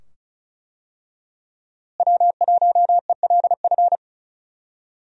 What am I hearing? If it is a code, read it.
Morse code, heard as a keyed tone, about 36 wpm, 700 Hz. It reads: W1ELF
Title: USA Call Signs - 5 Characters - Repeated 2x - 35wpm